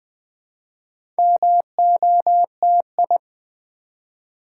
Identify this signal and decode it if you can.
Morse code, heard as a keyed tone, about 20 wpm, 700 Hz. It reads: MOTI
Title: Build A Word - 4 Letters At A Time - 10-Letter Words - 20wpm